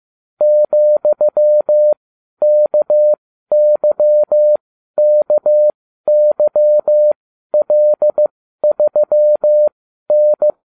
Morse code, heard as a keyed tone, about 15 wpm, spaced as for 12 wpm, 600 Hz. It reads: ,KYKYL3N